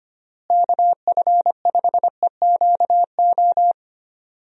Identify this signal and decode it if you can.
Morse code, heard as a keyed tone, about 25 wpm, 700 Hz. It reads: KF5EQO
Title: USA Call Signs - 6 Characters - Repeated 2x - 25wpm